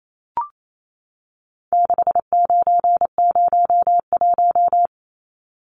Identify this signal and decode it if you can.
Morse code, heard as a keyed tone, about 28 wpm, 700 Hz. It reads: 6901
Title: Numbers - 4-Digit 28wpm